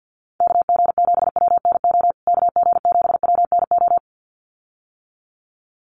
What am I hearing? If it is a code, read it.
Morse code, heard as a keyed tone, about 50 wpm, 700 Hz. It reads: KG7WNO KG7WNO
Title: USA Call Signs Repeated 2x 50wpm